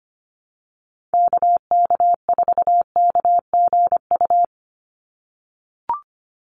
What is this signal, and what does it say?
Morse code, heard as a keyed tone, about 25 wpm, 700 Hz. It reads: KK4KGU
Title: USA Call Signs 25wpm